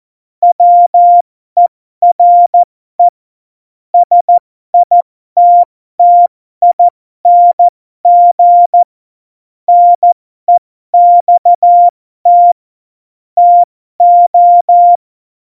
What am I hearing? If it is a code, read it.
Morse code, heard as a keyed tone, about 14 wpm, spaced as for 12 wpm, 700 Hz. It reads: WERE SITTING NEXT TO
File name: Morsekurs / Witze